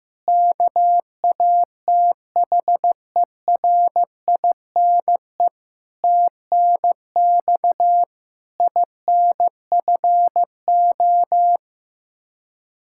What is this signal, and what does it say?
Morse code, heard as a keyed tone, about 16 wpm, 700 Hz. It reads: KATHERINE TNX INFO
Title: Practice QSO - Exchange 3 - More Info - 15wpm